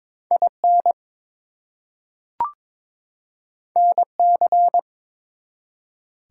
Morse code, heard as a keyed tone, about 22 wpm, 700 Hz. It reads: IN NC
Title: US States and Canadian Provinces 22wpm